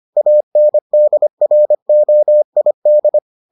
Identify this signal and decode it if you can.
Morse code, heard as a keyed tone, about 25 wpm, 600 Hz. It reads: ANDROID